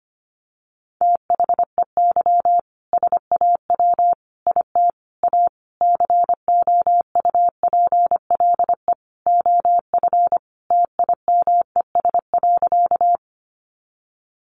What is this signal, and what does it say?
Morse code, heard as a keyed tone, about 25 wpm, 700 Hz. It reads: THEY SAW IT A COUPLE OF TIMES.
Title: Sentences from Top 1100 Words 25wpm